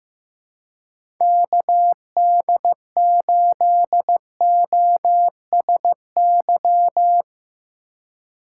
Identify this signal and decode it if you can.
Morse code, heard as a keyed tone, about 15 wpm, 700 Hz. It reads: KD8OSY
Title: USA Call Signs - 6 Characters - 2x Character Spacing - 15wpm